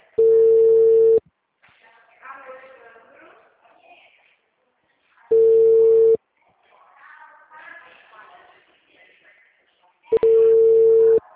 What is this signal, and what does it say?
Ringback tone (China, 450 Hz), with speech.